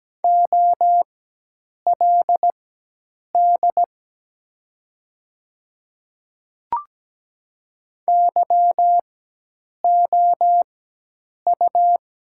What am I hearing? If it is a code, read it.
Morse code, heard as a keyed tone, about 17 wpm, spaced as for 8 wpm, 700 Hz. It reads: OLD YOU
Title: 4x Character Spacing - Top 200 Words - 17wpm